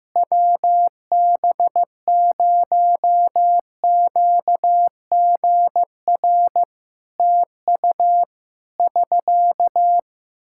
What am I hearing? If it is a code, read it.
Morse code, heard as a keyed tone, about 15 wpm, 700 Hz. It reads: WB0QGR TU <SK>
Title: Practice QSO - Exchange 4 - Ending - 15wpm